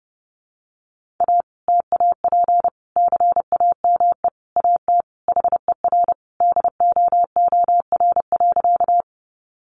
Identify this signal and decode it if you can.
Morse code, heard as a keyed tone, about 30 wpm, 700 Hz. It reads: A TAP CAME AT HER DOOR.
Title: Sentences from Top 1400 Words 30wpm